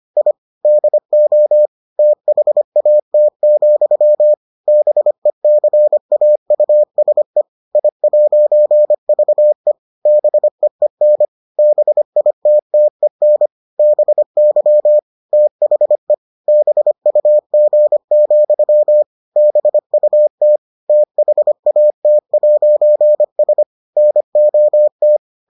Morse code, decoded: I DO THAT, BECAUSE I'VE BEEN BITTEN BY THE BUG, BUT THAT'S NOT

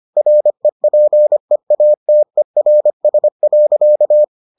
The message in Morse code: REPEATERS.

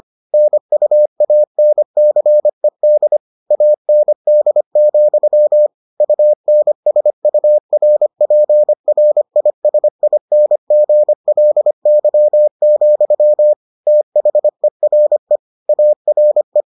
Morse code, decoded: NUANCED AND, UNSURPRISINGLY, THERE ARE

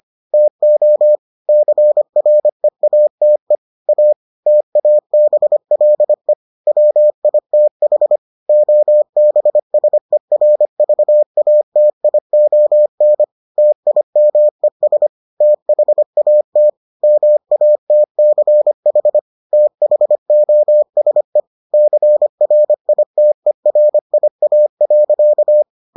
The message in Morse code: TO CREATE A TABLE WITH OBSERVATION TIMES THAT MATCH THOSE CRITERIA.

